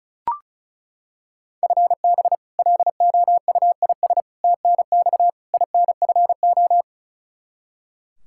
Morse code, heard as a keyed tone, about 34 wpm, 700 Hz. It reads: FB LOUIS TNX INFO
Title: Practice QSO - Exchange 3 - More Info - 35wpm